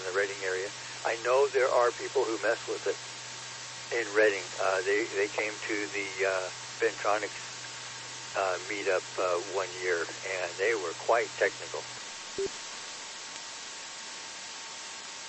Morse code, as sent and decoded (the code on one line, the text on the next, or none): none